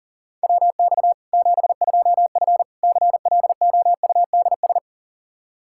Morse code, heard as a keyed tone, about 37 wpm, 700 Hz. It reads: WX 72F CLOUDS
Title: K1USN SST QSO Practice 40wpm